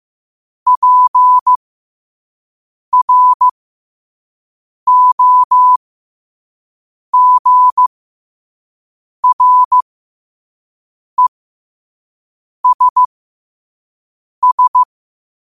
Morse code, heard as a keyed tone, about 15 wpm, spaced as for 5 wpm, 1000 Hz. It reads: PROGRESS
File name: qotdfiles / 5wpm